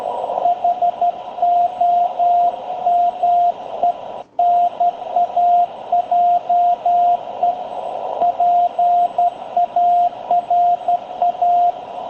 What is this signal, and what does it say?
Morse code, heard as a keyed tone, about 12 wpm, 700 Hz, with radio noise. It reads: HOMENAJE PARA
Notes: Encuentro de radioaficionados en 7030, banda de 40 m, batiendo manipuladores, para despedir a Tony, CA2LQA, de Villa Alemana, Valparaíso, Chile.